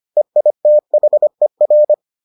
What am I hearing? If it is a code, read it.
Morse code, heard as a keyed tone, about 25 wpm, 600 Hz. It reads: EITHER